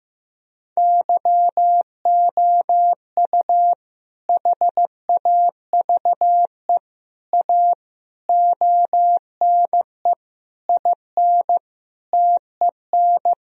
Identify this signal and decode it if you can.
Morse code, heard as a keyed tone, about 15 wpm, 700 Hz. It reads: YOU HAVE A ONE IN TEN
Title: Sentences from Top 1000 Words - Review Part 2 - 15wpm